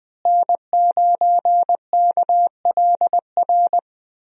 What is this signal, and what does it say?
Morse code, heard as a keyed tone, about 20 wpm, 700 Hz. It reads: N9KLR
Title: USA Call Signs - 5 Characters - Repeated 3x - 20wpm